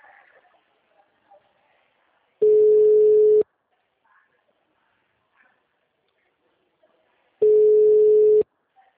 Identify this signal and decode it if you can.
Ringback tone (Europe, 425 Hz), with speech.